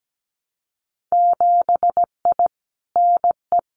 Morse code, heard as a keyed tone, about 17 wpm, 700 Hz. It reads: 7I NE
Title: Winter Field Day Exchanges Encore 17wpm